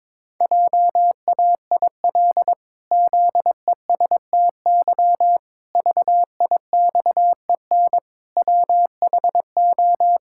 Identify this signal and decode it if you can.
Morse code, heard as a keyed tone, about 22 wpm, 700 Hz. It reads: JAIL ZESTY VIXEN WHO
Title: Pangram Challenge v2 - 22wpm